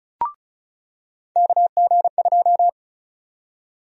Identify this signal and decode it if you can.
Morse code, heard as a keyed tone, about 36 wpm, 700 Hz. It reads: KG2
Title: USA Call Sign Prefixes - 35wpm